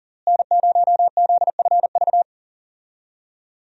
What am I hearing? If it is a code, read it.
Morse code, heard as a keyed tone, about 40 wpm, 700 Hz. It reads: N0ZFV
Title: USA Call Signs - 5 Characters - Repeated 2x - 40wpm